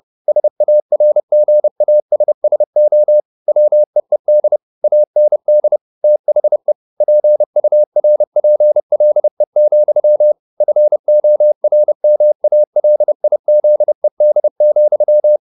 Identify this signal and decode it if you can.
Morse code, heard as a keyed tone, about 30 wpm, 600 Hz. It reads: SARGASSO WEED AND THE PURPLE, FORMALIZED,
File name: the old man and the sea / the old man and the sea 1 30wpm